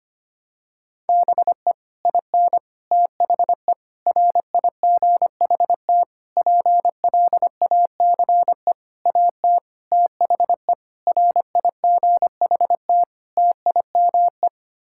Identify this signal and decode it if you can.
Morse code, heard as a keyed tone, about 25 wpm, 700 Hz. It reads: BE IN THE RIGHT PLACE AT THE RIGHT TIME
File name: Sets of 9 Words - 25wpm